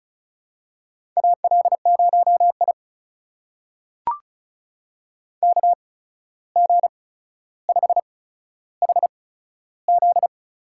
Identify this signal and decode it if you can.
Morse code, heard as a keyed tone, about 36 wpm, spaced as for 14 wpm, 700 Hz. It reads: AL0I KG5HZ
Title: USA Call Signs - 8x Character Spacing - 35wpm